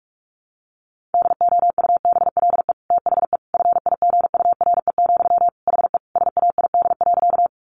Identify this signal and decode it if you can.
Morse code, heard as a keyed tone, about 44 wpm, 700 Hz. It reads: DOUBLE THE FIGURE, HE SAID.
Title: Sentences from Top 1400 Words 45wpm